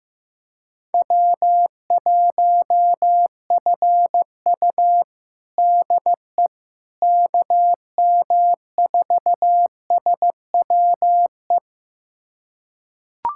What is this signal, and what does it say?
Morse code, heard as a keyed tone, about 15 wpm, 700 Hz. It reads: W1FU DE KM4SWE
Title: USA Callsign DE Callsign 15wpm